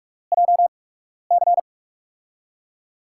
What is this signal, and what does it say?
Morse code, heard as a keyed tone, about 47 wpm, spaced as for 12 wpm, 700 Hz. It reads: JC
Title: USA Call Signs - 5 Characters - 8x Character Spacing - 45wpm